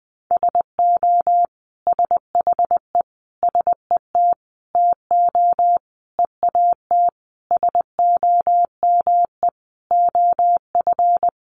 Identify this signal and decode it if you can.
Morse code, heard as a keyed tone, about 20 wpm, 700 Hz. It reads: SO SHE SET TO EAT SOME OF
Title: Sentences from Top 900 Words 20wpm